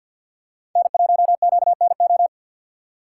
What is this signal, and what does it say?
Morse code, heard as a keyed tone, about 47 wpm, 700 Hz. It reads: N1QNO
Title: USA Call Signs - 5 Characters - 3x Character Spacing - 50wpm